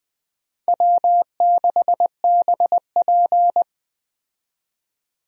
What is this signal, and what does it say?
Morse code, heard as a keyed tone, about 20 wpm, 700 Hz. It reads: W6BP
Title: USA Call Signs - 4 Characters - 20wpm